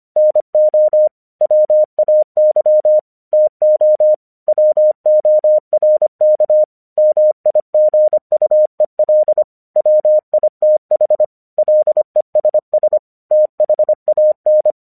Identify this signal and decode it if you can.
Morse code, heard as a keyed tone, about 25 wpm, 610 Hz. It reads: NO WAY TO WORK MIGUEL WITH LESS THAN
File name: 1 English 25wpm / AQC32